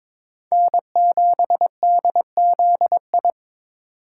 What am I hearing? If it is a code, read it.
Morse code, heard as a keyed tone, about 22 wpm, 700 Hz. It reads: N7DZI
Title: USA Call Signs - 5 Characters - 2x Character Spacing - 22wpm